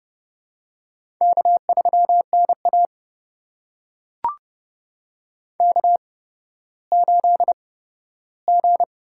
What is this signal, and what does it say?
Morse code, heard as a keyed tone, about 30 wpm, spaced as for 14 wpm, 700 Hz. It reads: K3NA K8G
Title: USA Call Signs - 8x Character Spacing - 30wpm